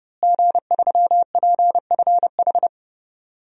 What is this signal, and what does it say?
Morse code, heard as a keyed tone, about 32 wpm, 700 Hz. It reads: G3PFH
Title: DXCC-100 Call Signs - 5 Characters - Repeated 3x - 30wpm